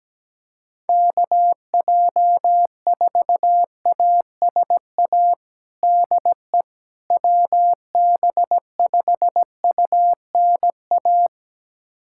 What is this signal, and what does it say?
Morse code, heard as a keyed tone, about 17 wpm, 700 Hz. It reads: KJ4ASA DE WB5UNA
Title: USA Callsign DE Callsign - 6 Characters - 17wpm